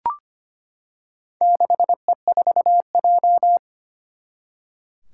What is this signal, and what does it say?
Morse code, heard as a keyed tone, about 25 wpm, 700 Hz. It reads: 6E4J
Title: Four Challenging Letter-Number Combinations 25wpm